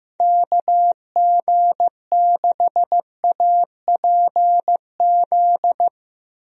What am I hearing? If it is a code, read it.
Morse code, heard as a keyed tone, about 15 wpm, 700 Hz. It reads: KG6APZ
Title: USA Call Signs - 6 Characters - Repeated 3x - 15wpm